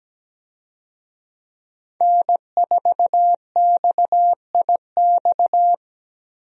Morse code, heard as a keyed tone, about 17 wpm, 700 Hz. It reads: N4XIX
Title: USA Call Signs - 5 Characters - 17wpm